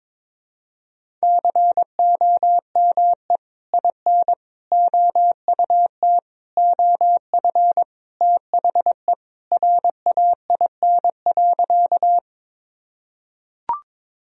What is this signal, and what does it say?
Morse code, heard as a keyed tone, about 21 wpm, 700 Hz. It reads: COME IN OUT OF THE RAIN.
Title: Sentences from Top 1000 Words - Review Part 1 - 22wpm